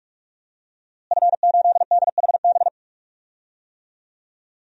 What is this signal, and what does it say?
Morse code, heard as a keyed tone, about 45 wpm, 700 Hz. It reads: F8DHB
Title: DXCC-100 Call Signs - 5 Characters - Repeated 3x - 45wpm